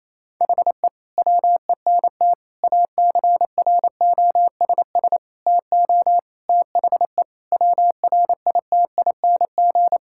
Morse code, heard as a keyed tone, about 28 wpm, 700 Hz. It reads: HE WENT ACROSS TO THE WRITING
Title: Sentences from Top 1000 Words - Review Part 5 - 28wpm